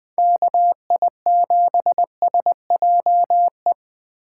Morse code, heard as a keyed tone, about 21 wpm, 700 Hz. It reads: KI7SJE
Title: USA Call Signs - 6 Characters - 2x Character Spacing - 20wpm